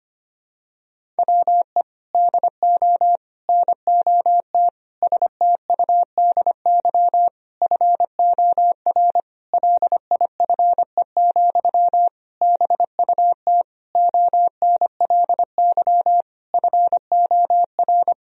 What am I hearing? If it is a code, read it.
Morse code, heard as a keyed tone, about 25 wpm, 700 Hz. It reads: WE DO NOT STUDY FOR LIFE, BUT ONLY FOR
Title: Sentences from Top 1000 Words - Review Part 1 - 25wpm